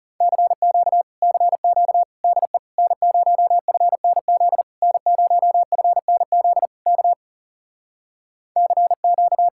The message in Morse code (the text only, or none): CQ CQ DE N0FNZ N0FNZ K CQ